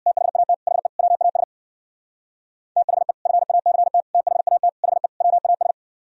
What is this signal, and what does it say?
Morse code, heard as a keyed tone, about 60 wpm, 700 Hz. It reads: THAT HE WAS THE FACT THAT HE WAS
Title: Speed-Racing - Sets of 5 Words 50wpm